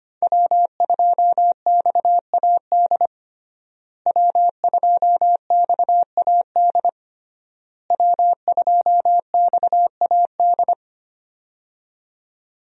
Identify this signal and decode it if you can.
Morse code, heard as a keyed tone, about 25 wpm, 700 Hz. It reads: W2XAD W2XAD W2XAD
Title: USA Call Signs - 5 Characters - Repeated 3x - 25wpm